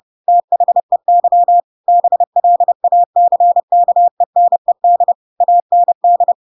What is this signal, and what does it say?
Morse code, heard as a keyed tone, about 30 wpm, 700 Hz. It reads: THEY BLACKENED AND
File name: Pinocchio / Pinocchio ch6 30 30 ik1hge